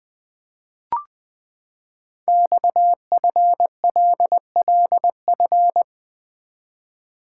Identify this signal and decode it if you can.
Morse code, heard as a keyed tone, about 20 wpm, 700 Hz. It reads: XFLLF
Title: Code Groups - 5 Letters - 20wpm